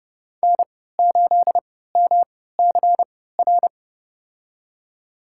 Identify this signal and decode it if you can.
Morse code, heard as a keyed tone, about 30 wpm, spaced as for 17 wpm, 700 Hz. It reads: N8MCR
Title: USA Call Signs - 5 Characters - 3x Character Spacing - 30wpm